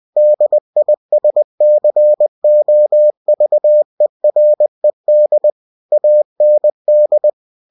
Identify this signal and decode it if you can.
Morse code, heard as a keyed tone, about 20 wpm, 600 Hz. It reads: DISCOVERED AND
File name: cw-books / winnie-the-pooh / winnie-20wpm-page-0117